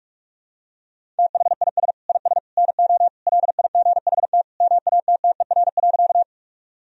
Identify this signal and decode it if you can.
Morse code, heard as a keyed tone, about 46 wpm, 700 Hz. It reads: THIS IS NO LIGHT MATTER.
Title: Sentences from Top 1000 Words - Review Part 1 - 45wpm